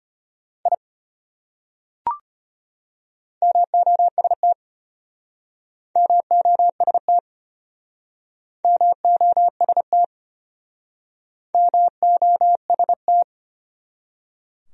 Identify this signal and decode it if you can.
Morse code, heard as a keyed tone, about 31 wpm, 700 Hz. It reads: I MOST MOST MOST MOST
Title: Speed-Racing Top 100 Words 25wpm